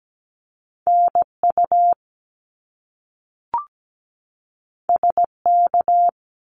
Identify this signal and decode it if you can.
Morse code, heard as a keyed tone, about 17 wpm, 700 Hz. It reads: NU SK
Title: Canadian Provinces and Territories 17wpm